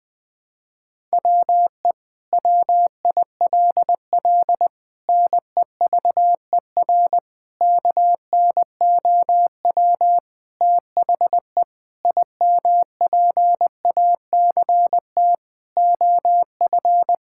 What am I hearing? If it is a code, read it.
Morse code, heard as a keyed tone, about 20 wpm, 700 Hz. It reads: WE WILL NEVER KNOW THE IMPACT OF
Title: Sentences from Top 1000 Words - Review Part 6 - 20wpm